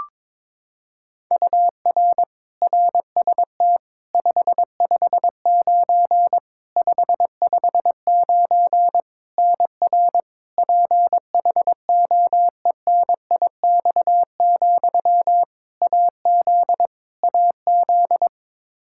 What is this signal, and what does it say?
Morse code, heard as a keyed tone, about 22 wpm, 700 Hz. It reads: UR RST 559 559 NR PHOENIX, AZ AZ
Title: Practice QSO - Exchange 1 - Beginning - 22wpm